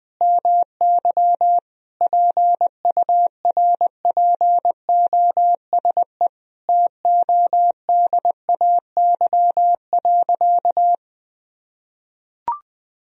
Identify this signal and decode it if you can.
Morse code, heard as a keyed tone, about 21 wpm, 700 Hz. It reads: MY PURPOSE TODAY.
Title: Sentences from Top 1000 Words - Review Part 4 - 20wpm